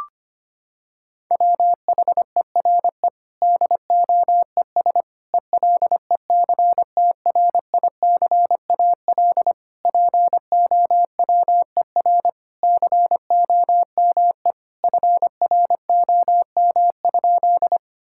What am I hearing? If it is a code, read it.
Morse code, heard as a keyed tone, about 25 wpm, 700 Hz. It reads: WHERE DOES ELECTRICAL POWER COME FROM?
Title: Sentences from Top 1000 Words - Review Part 6 - 25wpm